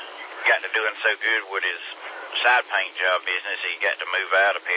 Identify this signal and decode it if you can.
Emergency Alert System (SAME) tones, heard as harsh buzzing bursts, with speech. It is heard from 0.8 to 4.5 s.